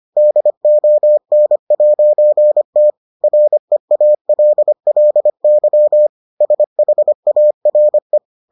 Morse code, heard as a keyed tone, about 25 wpm, 600 Hz. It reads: DON'T REALLY SHARE